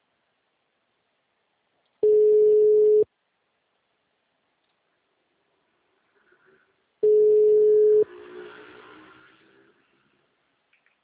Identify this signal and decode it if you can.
Ringback tone (Europe, 425 Hz).
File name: out- -2006- - - .42